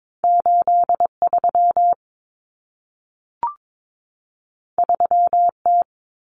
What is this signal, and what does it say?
Morse code, heard as a keyed tone, about 22 wpm, 700 Hz. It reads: 83 3T
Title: Common Cut Numbers - 2-Digit 22wpm